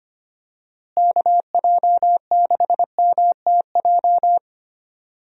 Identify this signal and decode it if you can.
Morse code, heard as a keyed tone, about 25 wpm, 700 Hz. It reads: KJ6MTJ
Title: USA Call Signs - 6 Characters - Repeated 3x - 25wpm